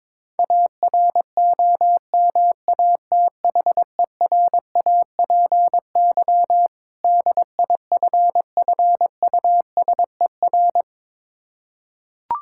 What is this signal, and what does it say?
Morse code, heard as a keyed tone, about 22 wpm, 690 Hz. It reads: AROMATHERAPY DIFFUSER
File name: Difficult Words with Hints - 22wpm